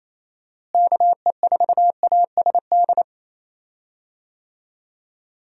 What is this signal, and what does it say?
Morse code, heard as a keyed tone, about 28 wpm, 700 Hz. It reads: KE4ASD
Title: USA Call Signs - 6 Characters - Repeated 2x - 28wpm